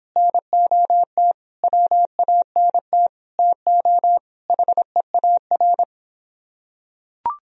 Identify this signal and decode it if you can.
Morse code, heard as a keyed tone, about 24 wpm, 700 Hz. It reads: NOT WANT TO HEAR